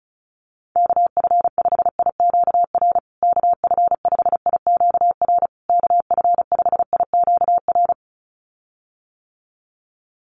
Morse code, heard as a keyed tone, about 34 wpm, 700 Hz. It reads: KF5IQR KF5IQR KF5IQR
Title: USA Call Signs Repeated 3x 35wpm